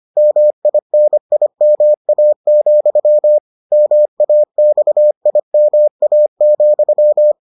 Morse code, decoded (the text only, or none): MINIMA, MAXIMA,